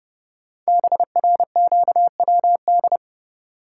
Morse code, heard as a keyed tone, about 30 wpm, 700 Hz. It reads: BRQWD
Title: Code Groups - 5 Letters - 30wpm